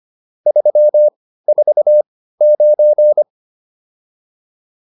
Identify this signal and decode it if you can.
Morse code, heard as a keyed tone, about 25 wpm, spaced as for 15 wpm, 600 Hz. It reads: 349
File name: QSO 408 15